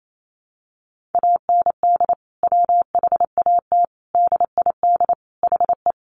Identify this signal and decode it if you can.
Morse code, heard as a keyed tone, about 28 wpm, 700 Hz. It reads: AND WHAT DID HE
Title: Sentences from Top 200 Words 28wpm